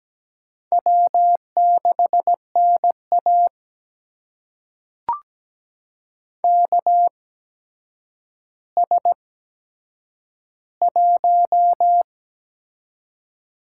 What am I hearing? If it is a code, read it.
Morse code, heard as a keyed tone, about 17 wpm, spaced as for 8 wpm, 700 Hz. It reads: W6NA KS1
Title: USA Call Signs - 4 Characters - 8x Character Spacing - 17wpm